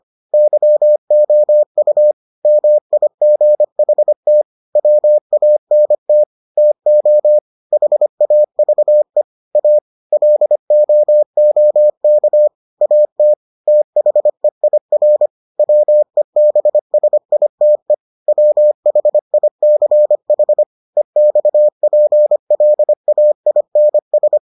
Morse code, heard as a keyed tone, about 25 wpm, 600 Hz. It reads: YOU MIGHT WANT TO HAVE A LOOK AT THEIR WEBSITE WHICH EXPLAINS